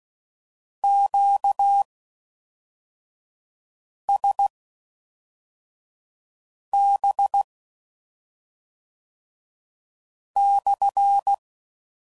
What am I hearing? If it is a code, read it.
Morse code, heard as a keyed tone, about 16 wpm, spaced as for 3 wpm, 800 Hz. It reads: QSB/